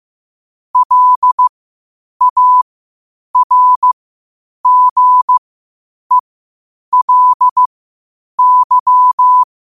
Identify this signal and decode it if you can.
Morse code, heard as a keyed tone, about 15 wpm, spaced as for 8 wpm, 1000 Hz. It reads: LARGELY